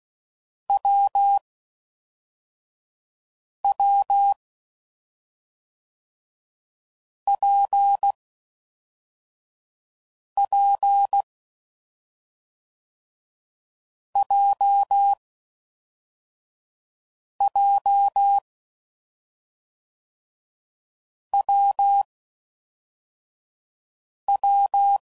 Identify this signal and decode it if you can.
Morse code, heard as a keyed tone, about 16 wpm, 800 Hz. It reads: W W P P J J W W
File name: Combinations / Difficult Combos / WPJ WPJ1